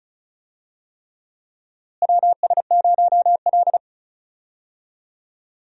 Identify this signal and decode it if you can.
Morse code, heard as a keyed tone, about 36 wpm, 700 Hz. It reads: WS0L
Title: USA Call Signs - 4 Characters - Repeated 3x - 35wpm